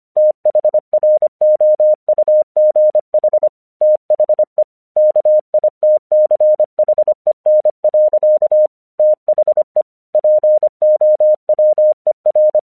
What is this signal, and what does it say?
Morse code, heard as a keyed tone, about 25 wpm, 610 Hz. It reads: THROUGH THE KITCHEN. THE POWER
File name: 1 English 25wpm / AQC16